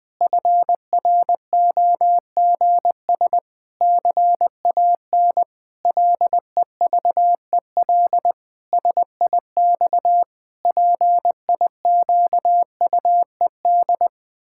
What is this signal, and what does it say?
Morse code, heard as a keyed tone, about 21 wpm, 700 Hz. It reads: FROGS CAN LEVEL SIX PIQUED
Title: Pangram Challenge v2 - 20wpm